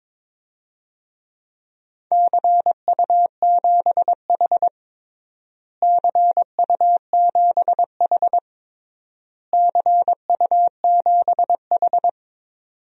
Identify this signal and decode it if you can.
Morse code, heard as a keyed tone, about 22 wpm, 700 Hz. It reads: CU7H CU7H CU7H
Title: DXCC-100 Call Signs - 4 Characters - Repeated 3x - 22wpm